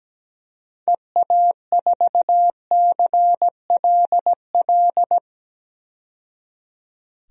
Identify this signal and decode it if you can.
Morse code, heard as a keyed tone, about 17 wpm, 700 Hz. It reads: EA4CLL
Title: DXCC-100 Call Signs - 6 Characters - 17wpm